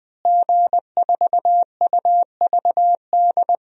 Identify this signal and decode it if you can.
Morse code, heard as a keyed tone, about 20 wpm, 700 Hz. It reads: G4UVD
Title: DXCC-100 Call Signs - 5 Characters - Repeated 2x - 20wpm